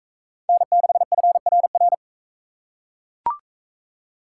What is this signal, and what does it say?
Morse code, heard as a keyed tone, about 40 wpm, 700 Hz. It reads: N6FRR